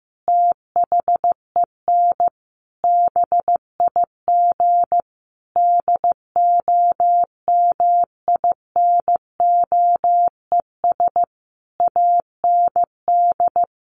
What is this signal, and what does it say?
Morse code, heard as a keyed tone, about 15 wpm, 700 Hz. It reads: THEN BIG DOMINOES AND
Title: A Sound Of Thunder 15wpm